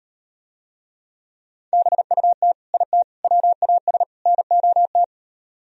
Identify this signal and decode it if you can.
Morse code, heard as a keyed tone, about 38 wpm, 700 Hz. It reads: BUT IT WAS NOT